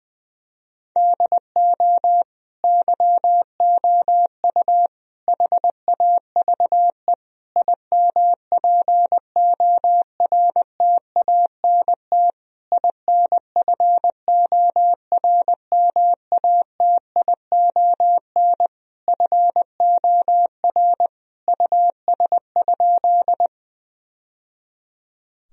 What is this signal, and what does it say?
Morse code, heard as a keyed tone, about 20 wpm, 700 Hz. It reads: DO YOU HAVE IMPORTANT INFORMATION FOR US?
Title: Sentences from Top 1000 Words - Review Part 3 - 20wpm